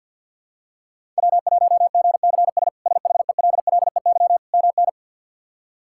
Morse code, heard as a keyed tone, about 52 wpm, 700 Hz. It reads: W1GCS SHELLEY MN
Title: K1USN SST QSO Practice - 50wpm